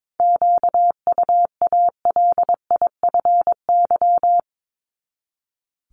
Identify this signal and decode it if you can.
Morse code, heard as a keyed tone, about 22 wpm, 700 Hz. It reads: QUALIFY
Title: Top 1751-2000 Words 22wpm